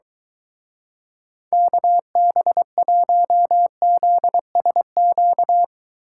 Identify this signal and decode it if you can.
Morse code, heard as a keyed tone, about 23 wpm, 700 Hz. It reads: KB1ZSQ